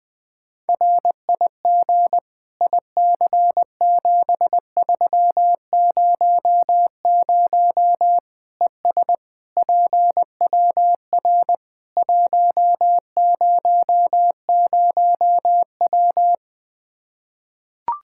Morse code, heard as a keyed tone, about 21 wpm, 700 Hz. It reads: RIG IC7300 ES PWR 100W
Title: Practice QSO - Exchange 2 - Info - 20wpm